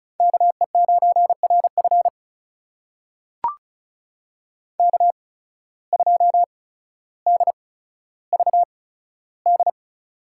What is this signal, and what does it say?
Morse code, heard as a keyed tone, about 34 wpm, spaced as for 15 wpm, 700 Hz. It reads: KE9RF K2DVD
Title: USA Call Signs - 5 Characters - 8x Character Spacing - 35wpm